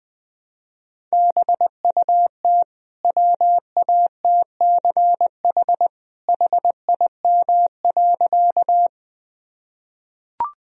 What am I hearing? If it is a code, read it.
Morse code, heard as a keyed tone, about 20 wpm, 700 Hz. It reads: BUT WATCH HIM.
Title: Sentences from Top 1000 Words - Review Part 1 - 20wpm